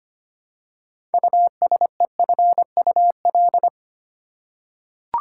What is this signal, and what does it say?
Morse code, heard as a keyed tone, about 25 wpm, 700 Hz. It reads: USEFUL
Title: Six Letter Words - 25wpm